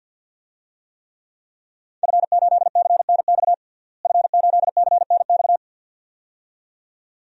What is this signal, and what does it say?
Morse code, heard as a keyed tone, about 47 wpm, 700 Hz. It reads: F8CNX F8CNX
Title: DXCC-100 Call Signs - 5 Characters - Repeated 2x - 50wpm